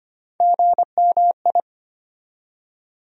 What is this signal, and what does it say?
Morse code, heard as a keyed tone, about 25 wpm, 700 Hz. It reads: GMI
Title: USA Call Sign Postfixes - 3 Letters 25wpm